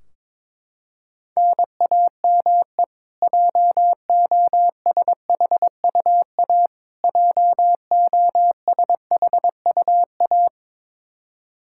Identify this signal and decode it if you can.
Morse code, heard as a keyed tone, about 22 wpm, 700 Hz. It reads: NAME JOSHUA JOSHUA
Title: Practice QSO - Exchange 1 - Beginning - 22wpm